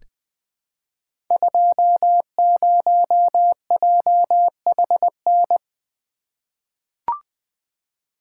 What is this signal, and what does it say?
Morse code, heard as a keyed tone, about 20 wpm, 700 Hz. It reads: 20JHN
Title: Code Groups - 5 Letter:Numbers - 20wpm